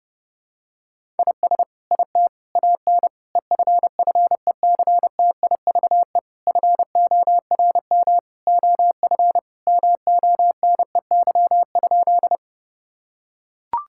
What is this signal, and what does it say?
Morse code, heard as a keyed tone, about 30 wpm, 700 Hz. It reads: IS IT AN EFFECTIVE FORM OF MONEY?
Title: Sentences from Top 1000 Words - Review Part 6 - 30wpm